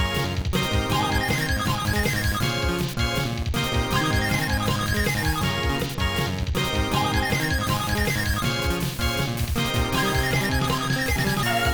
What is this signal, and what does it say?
There is no code.